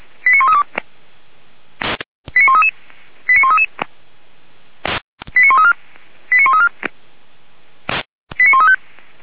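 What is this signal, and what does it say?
Five-tone selective call (ZVEI), heard as quick runs of five tones, tones of about 80 ms. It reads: 89132 89133 89133 89134 89134 89135
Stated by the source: ZVEI five-tone selective call received on an SDR.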